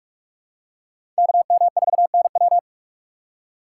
Morse code, heard as a keyed tone, about 44 wpm, 700 Hz. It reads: KM4NW
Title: USA Call Signs - 5 Characters - 4x Character Spacing - 45wpm